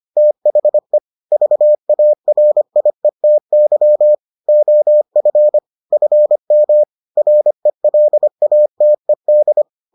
Morse code, heard as a keyed tone, about 25 wpm, 600 Hz. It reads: THE VARIETY OF FM RELATED